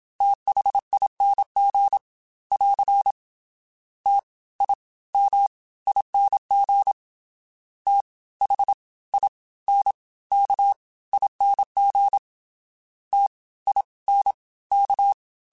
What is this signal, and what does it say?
Morse code, heard as a keyed tone, about 27 wpm, spaced as for 19 wpm, 800 Hz. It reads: THING <AR> TIMING THINKING TINK